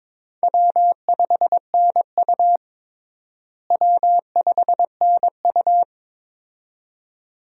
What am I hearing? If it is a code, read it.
Morse code, heard as a keyed tone, about 22 wpm, 700 Hz. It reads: W5NU W5NU
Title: USA Call Signs - 4 Characters - Repeated 2x - 22wpm